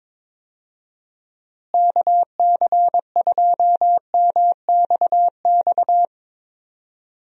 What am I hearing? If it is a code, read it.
Morse code, heard as a keyed tone, about 22 wpm, 700 Hz. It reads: KC2MXX
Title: USA Call Signs - 6 Characters - Repeated 3x - 22wpm